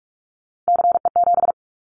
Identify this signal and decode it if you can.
Morse code, heard as a keyed tone, about 44 wpm, 700 Hz. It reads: CE7
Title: Callsign Prefixes for Top 100 DXCC - 45wpm